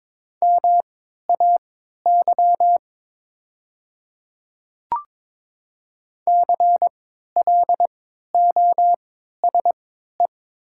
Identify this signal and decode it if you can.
Morse code, heard as a keyed tone, about 22 wpm, spaced as for 13 wpm, 700 Hz. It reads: MAY CLOSE
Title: 3x Character Spacing - Top 300 Words - 22wpm